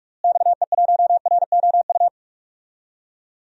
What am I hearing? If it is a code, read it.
Morse code, heard as a keyed tone, about 46 wpm, 700 Hz. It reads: XE1ROU
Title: DXCC-100 Call Signs - 6 Characters - Repeated 3x - 45wpm